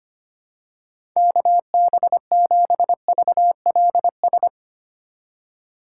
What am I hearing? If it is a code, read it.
Morse code, heard as a keyed tone, about 25 wpm, 700 Hz. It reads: KB7VLS
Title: USA Call Signs - 6 Characters - 2x Character Spacing - 25wpm